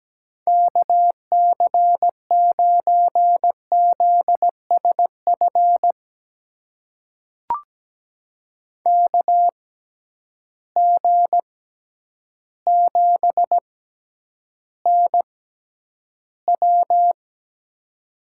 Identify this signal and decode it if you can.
Morse code, heard as a keyed tone, about 17 wpm, spaced as for 9 wpm, 700 Hz. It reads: KC9ZSF KG7NW
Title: USA Call Signs - 6x Character Spacing - 17wpm